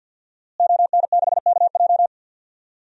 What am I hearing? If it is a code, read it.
Morse code, heard as a keyed tone, about 52 wpm, 700 Hz. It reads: ON6KJ
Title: DXCC-100 Call Signs - 5 Characters - Repeated 2x - 50wpm